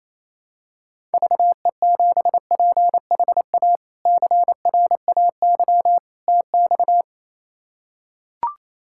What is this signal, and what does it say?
Morse code, heard as a keyed tone, about 28 wpm, 700 Hz. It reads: VE7PHA CRAY TX
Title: K1USN SST QSO Practice - 28wpm